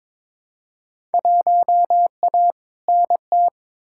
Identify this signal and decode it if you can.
Morse code, heard as a keyed tone, about 21 wpm, 700 Hz. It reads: 1A NT
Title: Field Day Exchanges Encore - 22wpm